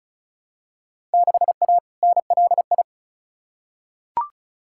Morse code, heard as a keyed tone, about 36 wpm, 690 Hz. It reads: 6A NLI
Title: Field Day Exchanges Encore - 35wpm